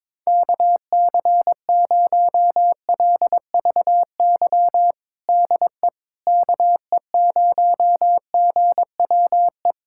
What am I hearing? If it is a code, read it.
Morse code, heard as a keyed tone, about 21 wpm, 700 Hz. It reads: KC0LVY DE KE0GWE
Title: USA Callsign DE Callsign 22wpm